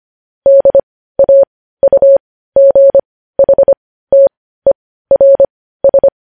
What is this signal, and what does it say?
Morse code, heard as a keyed tone, about 25 wpm, spaced as for 15 wpm, 550 Hz. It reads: DAUGHTERS
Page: news-headlines-in-morse-code-at-15-wpm